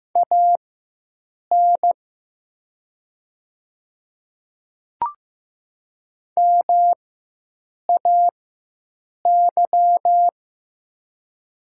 Morse code, heard as a keyed tone, about 15 wpm, spaced as for 7 wpm, 700 Hz. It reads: AN MAY
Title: 4x Character Spacing - Top 100 Words - 15wpm